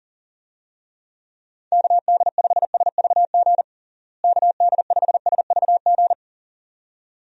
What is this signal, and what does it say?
Morse code, heard as a keyed tone, about 40 wpm, 700 Hz. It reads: KD5SVG KD5SVG
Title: USA Call Signs - 6 Characters - Repeated 2x - 40wpm